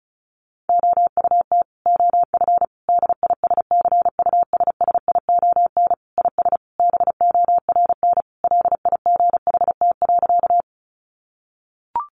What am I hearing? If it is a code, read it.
Morse code, heard as a keyed tone, about 34 wpm, 700 Hz. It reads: OUT OF DISCUSSION IS BORN LIGHT.
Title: Sentences from Top 1100 Words 35wpm